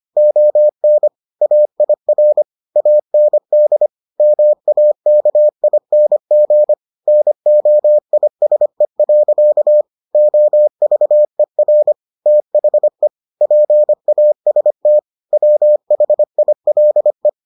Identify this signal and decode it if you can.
Morse code, heard as a keyed tone, about 25 wpm, 600 Hz. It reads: ON AIR AND MAKING NOISE. OVER THE PAST WHILE